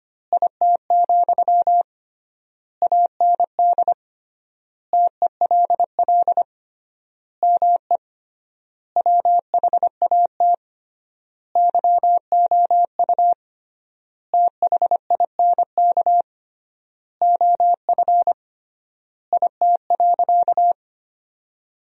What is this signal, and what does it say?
Morse code, heard as a keyed tone, about 25 wpm, 700 Hz. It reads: IT, AND TELL ME WHAT YOU THINK OF IT.
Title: 3x Word-Spacing - Sentences from Top 200 Words - 25wpm